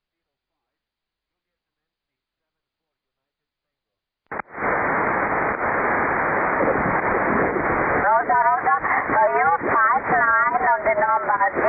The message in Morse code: none